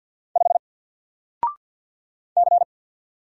Morse code, heard as a keyed tone, about 48 wpm, 700 Hz. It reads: 5 C
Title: Single Character (Letter:Number:Punctuation:Prosign) 50wpm